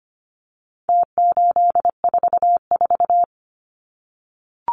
Morse code, heard as a keyed tone, about 25 wpm, 700 Hz. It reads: T844
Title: Numbers - Contest Style - 4-Digit 25wpm